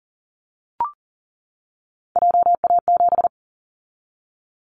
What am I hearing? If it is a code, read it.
Morse code, heard as a keyed tone, about 40 wpm, 700 Hz. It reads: JA7
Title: Callsign Prefixes for Top 25 DXCC - 40wpm